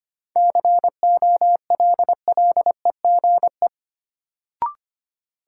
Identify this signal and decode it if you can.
Morse code, heard as a keyed tone, about 25 wpm, 700 Hz. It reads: COLLEGE
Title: Sentences from Top 1300 Words - New Word Drill Alt - 25wpm